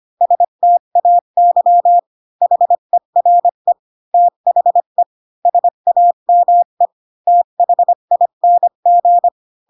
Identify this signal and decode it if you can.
Morse code, heard as a keyed tone, about 25 wpm, 700 Hz. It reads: STAY HERE THE SAME THING
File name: Pinocchio / Pinocchio ch4 20 25 ik1hge